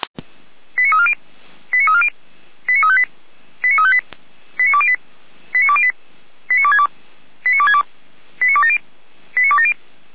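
Five-tone selective call (ZVEI), heard as quick runs of five tones, tones of about 70 ms. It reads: 89340 89340 89358 89358 89208 89208 89262 89262 89270 89270
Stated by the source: ZVEI five-tone selective call received on SDR.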